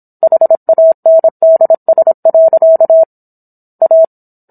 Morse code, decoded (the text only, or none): HANDS. A